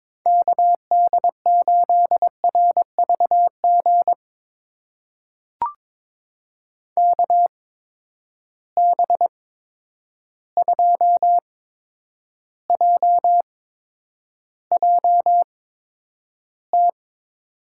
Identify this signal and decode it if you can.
Morse code, heard as a keyed tone, about 22 wpm, spaced as for 9 wpm, 700 Hz. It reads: KD8RVG KB2JJT
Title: USA Call Signs - 6 Characters - 8x Character Spacing - 22wpm